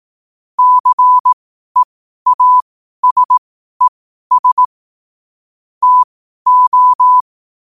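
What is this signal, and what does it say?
Morse code, heard as a keyed tone, about 18 wpm, spaced as for 12 wpm, 1000 Hz. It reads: CEASES TO
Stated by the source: Morse code at 12 WPM.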